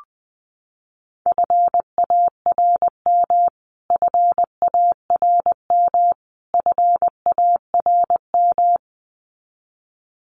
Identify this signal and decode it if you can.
Morse code, heard as a keyed tone, about 20 wpm, 700 Hz. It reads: FARM FARM FARM
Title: Top 500 Words Repeated 20wpm